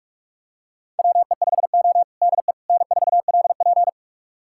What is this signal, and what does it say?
Morse code, heard as a keyed tone, about 44 wpm, 700 Hz. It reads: WE5O DE N4LP
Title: USA Callsign DE Callsign - 4 Characters - 45wpm